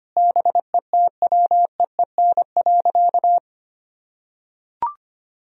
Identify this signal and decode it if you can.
Morse code, heard as a keyed tone, about 25 wpm, 700 Hz. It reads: BETWEEN.
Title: Sentences from Top 1000 Words - Review Part 1 - 25wpm